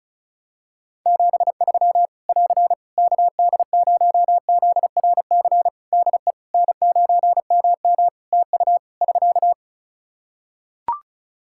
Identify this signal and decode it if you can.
Morse code, heard as a keyed tone, about 36 wpm, 700 Hz. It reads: 73 <AR> KD0ZRC DE N9MM TU <SK>
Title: Practice QSO - Exchange 4 - Ending - 35wpm